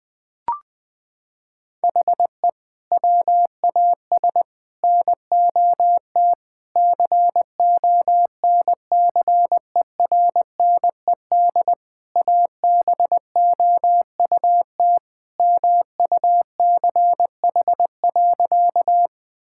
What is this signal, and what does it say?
Morse code, heard as a keyed tone, about 20 wpm, 700 Hz. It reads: HE WAS NOT CONCERNED ABOUT MUCH.
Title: Sentences from Top 1000 Words - Review Part 6 - 20wpm